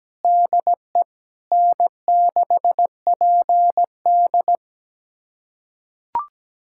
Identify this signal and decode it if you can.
Morse code, heard as a keyed tone, about 17 wpm, 700 Hz. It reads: DE N6PD
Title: USA Callsign DE Callsign - 4 Characters - 17wpm